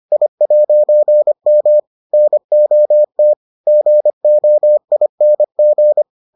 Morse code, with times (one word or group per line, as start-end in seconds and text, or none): 0.1-1.8 I'M
2.1-3.3 NOT
3.7-6.0 GOING